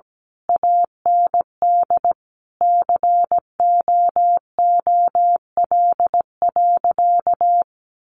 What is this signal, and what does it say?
Morse code, heard as a keyed tone, about 17 wpm, 700 Hz. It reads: AND COOL.
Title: Sentences from Top 1200 Words 17wpm